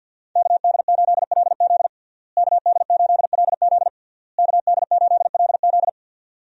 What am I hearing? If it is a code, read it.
Morse code, heard as a keyed tone, about 52 wpm, 700 Hz. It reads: KD8LZ KD8LZ KD8LZ
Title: USA Call Signs - 5 Characters - Repeated 3x - 50wpm